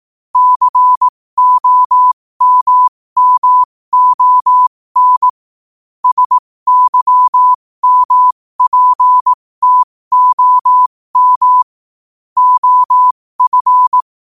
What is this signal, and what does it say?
Morse code, heard as a keyed tone, about 18 wpm, spaced as for 15 wpm, 1000 Hz. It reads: COMMON SYMPTOM OF
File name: qotdfiles / 15wpm